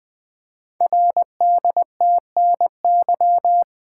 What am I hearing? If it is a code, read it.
Morse code, heard as a keyed tone, about 20 wpm, 700 Hz. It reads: RDTNY
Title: Code Groups - 5 Letters - 20wpm